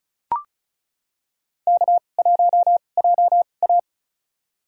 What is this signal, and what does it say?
Morse code, heard as a keyed tone, about 36 wpm, spaced as for 25 wpm, 700 Hz. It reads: K1JA
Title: USA Call Signs - 4 Characters - 2x Character Spacing - 35wpm